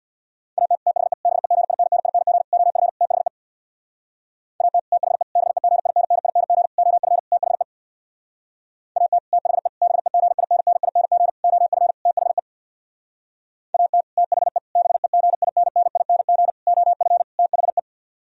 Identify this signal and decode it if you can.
Morse code, heard as a keyed tone, about 64 wpm, 700 Hz. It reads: AT THE BEGINNING OF THE AT THE BEGINNING OF THE AT THE BEGINNING OF THE AT THE BEGINNING OF THE